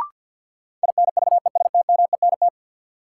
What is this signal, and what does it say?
Morse code, heard as a keyed tone, about 46 wpm, 700 Hz. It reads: INVESTMENT
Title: Sentences from Top 700 Words - New Word Drill - 50wpm